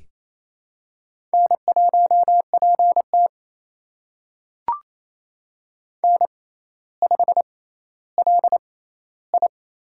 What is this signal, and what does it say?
Morse code, heard as a keyed tone, about 28 wpm, spaced as for 14 wpm, 700 Hz. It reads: N1PT N5LI
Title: USA Call Signs - 6x Character Spacing - 28wpm